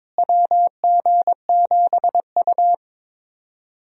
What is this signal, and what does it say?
Morse code, heard as a keyed tone, about 21 wpm, 700 Hz. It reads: WG7U
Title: USA Call Signs - 4 Characters - Repeated 3x - 22wpm